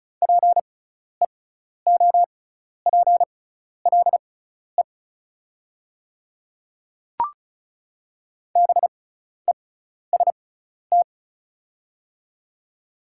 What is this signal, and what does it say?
Morse code, heard as a keyed tone, about 34 wpm, spaced as for 12 wpm, 700 Hz. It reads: PEOPLE BEST
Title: 6x Character Spacing - Top 500 Words - 35wpm